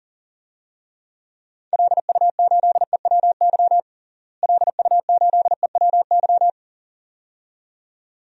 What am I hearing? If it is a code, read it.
Morse code, heard as a keyed tone, about 37 wpm, 700 Hz. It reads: LU8EWY LU8EWY
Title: DXCC-100 Call Signs - 6 Characters - Repeated 2x - 40wpm